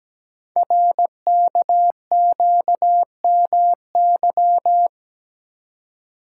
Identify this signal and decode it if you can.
Morse code, heard as a keyed tone, about 17 wpm, 700 Hz. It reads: RKQMY
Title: Code Groups - 5 Letter:Numbers - 17wpm